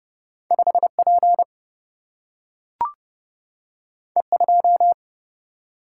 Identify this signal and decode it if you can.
Morse code, heard as a keyed tone, about 30 wpm, 700 Hz. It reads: 5P E2
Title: Single-Letter Single-Number 30wpm